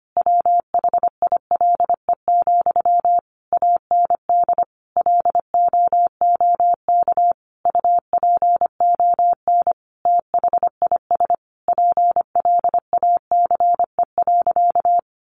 Morse code, heard as a keyed tone, about 25 wpm, 700 Hz. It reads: WHILE, AND LOOK UPON THIS PLACE.
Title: Sentences from Top 1100 Words 25wpm